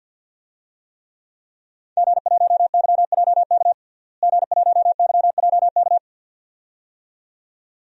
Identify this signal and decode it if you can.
Morse code, heard as a keyed tone, about 47 wpm, 700 Hz. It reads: G1YJK G1YJK
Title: DXCC-100 Call Signs - 5 Characters - Repeated 2x - 50wpm